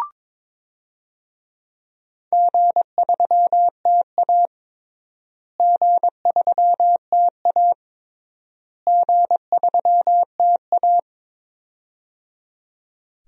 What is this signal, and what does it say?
Morse code, heard as a keyed tone, about 22 wpm, 700 Hz. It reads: G3TA G3TA G3TA
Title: DXCC-100 Call Signs - 4 Characters - Repeated 3x - 22wpm